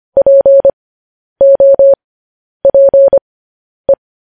Morse code, heard as a keyed tone, about 25 wpm, spaced as for 10 wpm, 550 Hz. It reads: POPE
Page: news-headlines-in-morse-code-at-10-wpm